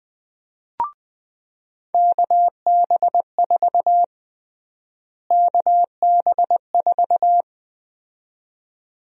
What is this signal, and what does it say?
Morse code, heard as a keyed tone, about 20 wpm, 700 Hz. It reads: KB4 KB4
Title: USA Call Sign Prefixes - Repeated 2x - 20wpm